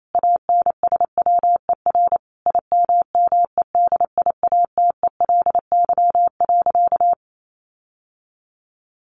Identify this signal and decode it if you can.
Morse code, heard as a keyed tone, about 28 wpm, 700 Hz. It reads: ANSWER IMMEDIATELY.
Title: Sentences from Top 1100 Words 28wpm